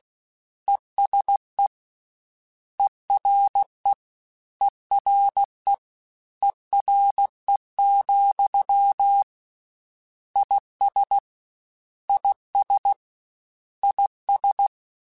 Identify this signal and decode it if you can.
Morse code, heard as a keyed tone, about 16 wpm, 800 Hz. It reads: ESE ERE ERE ERE, IS IS IS
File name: Combinations / Combinations-mp3s / Combinations@15